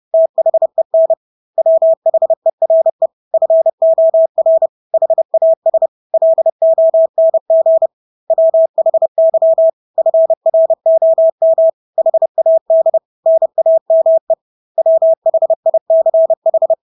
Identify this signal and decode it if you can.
Morse code, heard as a keyed tone, about 30 wpm, 650 Hz. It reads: THEN WHERE FOR HAS LONG WHY FROM HAD NAME WHICH
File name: Engelske ord 30-30 wpm